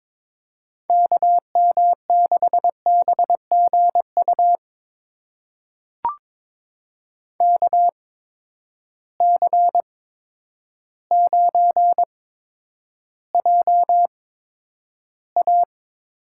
Morse code, heard as a keyed tone, about 22 wpm, spaced as for 10 wpm, 700 Hz. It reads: KM6BGU KC9JA
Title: USA Call Signs - 6 Characters - 8x Character Spacing - 22wpm